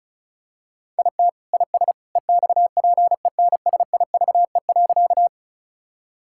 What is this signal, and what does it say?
Morse code, heard as a keyed tone, about 36 wpm, 700 Hz. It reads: IT IS EXPENSIVE.
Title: Sentences from Top 1000 Words - Review Part 6 - 35wpm